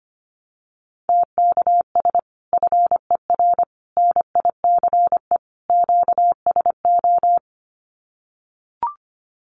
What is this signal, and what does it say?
Morse code, heard as a keyed tone, about 25 wpm, 700 Hz. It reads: TKS FER NICE QSO
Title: Common QSO Phrases 25wpm